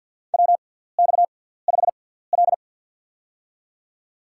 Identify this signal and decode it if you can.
Morse code, heard as a keyed tone, about 47 wpm, spaced as for 17 wpm, 700 Hz. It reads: WX5L
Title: USA Call Signs - 4 Characters - 6x Character Spacing - 50wpm